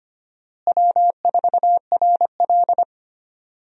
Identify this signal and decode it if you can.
Morse code, heard as a keyed tone, about 25 wpm, 700 Hz. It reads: W4RL
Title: USA Call Signs - 4 Characters - Repeated 2x - 25wpm